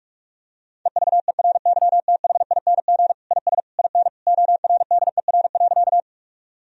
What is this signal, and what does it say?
Morse code, heard as a keyed tone, about 41 wpm, 700 Hz. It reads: EVERYTHING IS IN ORDER.
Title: Sentences from Top 1000 Words - Review Part 2 - 45wpm